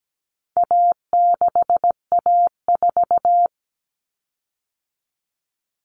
Morse code, heard as a keyed tone, about 17 wpm, 700 Hz. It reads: A6A4
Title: Common Cut Numbers - 4-Digit 17wpm